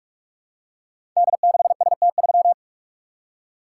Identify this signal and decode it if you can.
Morse code, heard as a keyed tone, about 44 wpm, 700 Hz. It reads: D6ST3
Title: Code Groups - 5 Letter:Numbers - 45wpm